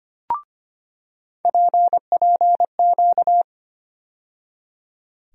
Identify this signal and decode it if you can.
Morse code, heard as a keyed tone, about 25 wpm, 700 Hz. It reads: PPQ
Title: USA Call Sign Postfixes - 3 Letters 25wpm